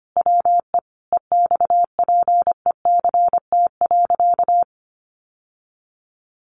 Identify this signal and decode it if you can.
Morse code, heard as a keyed tone, about 25 wpm, 700 Hz. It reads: WE EXPECT.
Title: Sentences from Top 1100 Words 25wpm